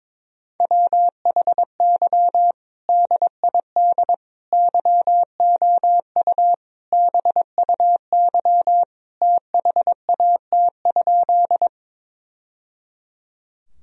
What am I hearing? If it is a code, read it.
Morse code, heard as a keyed tone, about 22 wpm, 700 Hz. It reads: WHY DID YOU BUY THAT?
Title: Sentences from Top 1000 Words - Review Part 4 - 22wpm